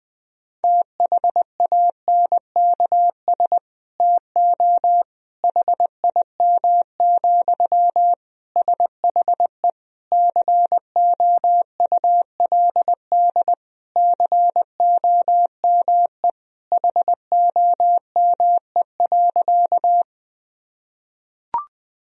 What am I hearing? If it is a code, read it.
Morse code, heard as a keyed tone, about 21 wpm, 700 Hz. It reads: THANKS TO HIM, SHE COULD COME HOME.
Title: Sentences from Top 1000 Words - Review Part 3 - 20wpm